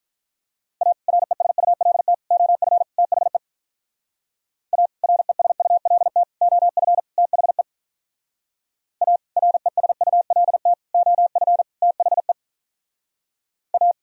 Morse code, heard as a keyed tone, about 44 wpm, 700 Hz. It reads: A RESULT OF THE A RESULT OF THE A RESULT OF THE A